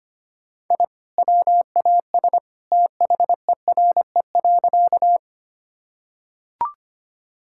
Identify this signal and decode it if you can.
Morse code, heard as a keyed tone, about 25 wpm, 700 Hz. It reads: I WAS THERE.
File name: practice / 25wpm / Sentences with 2-Word Phrases - 25wpm